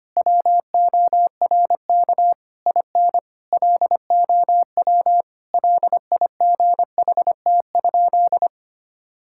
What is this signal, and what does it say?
Morse code, heard as a keyed tone, about 25 wpm, 700 Hz. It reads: WORK IN LOW LIGHT?
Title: Sentences from Top 1000 Words - Review Part 5 - 25wpm